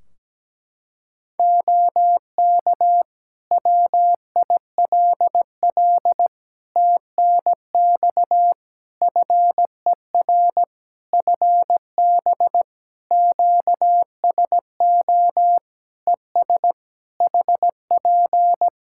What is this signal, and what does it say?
Morse code, heard as a keyed tone, about 17 wpm, 700 Hz. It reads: OK WILL TNX FER FB QSO ES HP
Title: Practice QSO - Exchange 4 - Ending - 17wpm